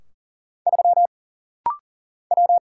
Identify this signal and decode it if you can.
Morse code, heard as a keyed tone, about 40 wpm, 700 Hz. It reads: 3 W
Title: Single Letter-Number - Rapid-Fire 40wpm